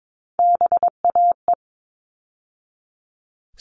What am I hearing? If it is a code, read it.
Morse code, heard as a keyed tone, about 22 wpm, 700 Hz. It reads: BAE
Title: All Cut Numbers - 3-Digit 22wpm